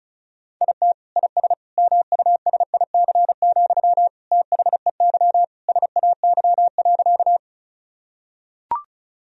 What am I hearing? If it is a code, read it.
Morse code, heard as a keyed tone, about 36 wpm, 700 Hz. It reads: IT IS MUSIC, THEY SAY.
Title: Sentences from Top 1000 Words - Review Part 1 - 35wpm